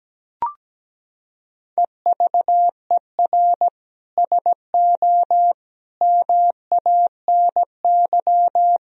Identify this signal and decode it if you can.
Morse code, heard as a keyed tone, about 17 wpm, 700 Hz. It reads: EVER SO MANY
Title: Sentences from Top 1000 Words - Review Part 3 - 17wpm